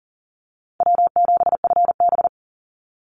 Morse code, heard as a keyed tone, about 40 wpm, 700 Hz. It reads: W7FB
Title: Easy Call Signs Repeated 3x - 40wpm